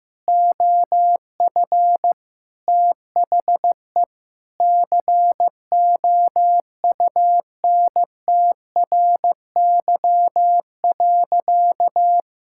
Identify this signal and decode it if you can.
Morse code, heard as a keyed tone, about 15 wpm, 700 Hz. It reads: OF THE COUNTRY.
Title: Sentences from Top 1000 Words - Review Part 6 - 15wpm